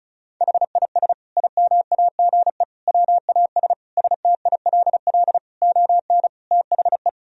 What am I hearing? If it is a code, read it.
Morse code, heard as a keyed tone, about 36 wpm, 700 Hz. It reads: HIS IMAGE WAS STILL ON THE
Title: Sentences from Top 1000 Words - Review Part 4 - 35wpm